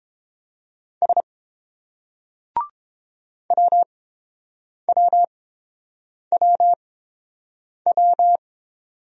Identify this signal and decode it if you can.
Morse code, heard as a keyed tone, about 31 wpm, spaced as for 7 wpm, 700 Hz. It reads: S WWWW